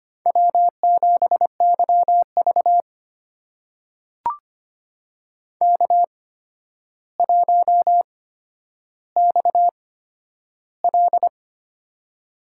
Morse code, heard as a keyed tone, about 25 wpm, spaced as for 11 wpm, 700 Hz. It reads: W7YV K1XL
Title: USA Call Signs - 4 Characters - 8x Character Spacing - 25wpm